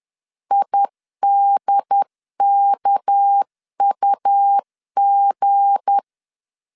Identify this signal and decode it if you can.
Morse code, heard as a keyed tone, about 10 wpm, 800 Hz. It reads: IDKUG
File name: LEZIONI / 7 GF9Z / 7 G 10wpm 2501